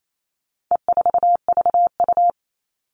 Morse code, heard as a keyed tone, about 28 wpm, 700 Hz. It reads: E4VU
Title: All Cut Numbers - 4-Digit 28wpm